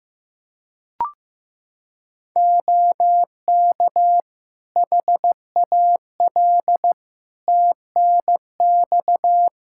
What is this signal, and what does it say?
Morse code, heard as a keyed tone, about 15 wpm, 700 Hz. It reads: OK HAL TNX
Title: Practice QSO - Exchange 4 - Ending - 15wpm